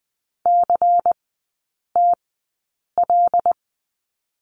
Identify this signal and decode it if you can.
Morse code, heard as a keyed tone, about 19 wpm, spaced as for 8 wpm, 700 Hz. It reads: CTL
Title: 127 Lesson 21e - Number 6 - Three Characters 20wpm